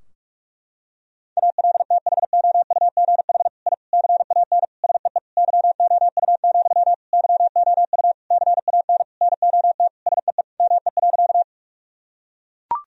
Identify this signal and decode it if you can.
Morse code, heard as a keyed tone, about 41 wpm, 700 Hz. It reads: ALTHOUGH I CAN SEE YOU, YOU CAN NOT SEE ME.
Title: Sentences from Top 1000 Words - Review Part 3 - 45wpm